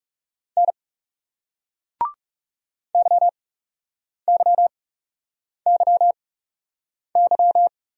Morse code, heard as a keyed tone, about 42 wpm, spaced as for 8 wpm, 700 Hz. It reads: N YYYY